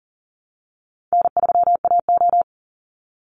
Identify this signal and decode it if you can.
Morse code, heard as a keyed tone, about 40 wpm, 700 Hz. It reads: N3AO
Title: Call Signs - 40wpm